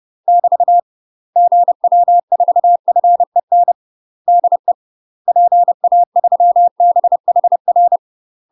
Morse code, heard as a keyed tone, about 30 wpm, 700 Hz. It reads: <BT> GW4FEN DE PA3BHR
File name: UK MASTER QSOs TRAINING / 30WPM / UK MASTER QSOs 0018